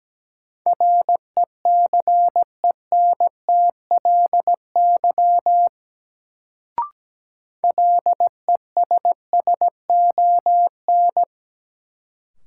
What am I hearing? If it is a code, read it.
Morse code, heard as a keyed tone, about 17 wpm, 700 Hz. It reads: RECENTLY LESSON
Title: Sentences from Top 1200 Words - New Word Drill - 17wpm